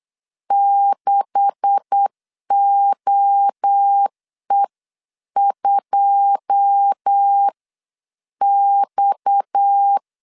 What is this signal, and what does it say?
Morse code, heard as a keyed tone, about 8 wpm, spaced as for 7 wpm, 800 Hz. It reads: 6OE2X
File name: CW / DET / 1 9 1500C S C 8wpm 5653